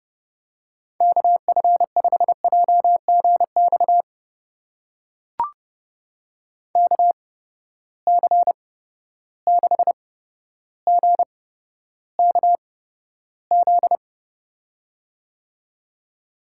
Morse code, heard as a keyed tone, about 32 wpm, spaced as for 13 wpm, 700 Hz. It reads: KF5JGX KC6GKZ
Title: USA Call Signs - 6 Characters - 8x Character Spacing - 30wpm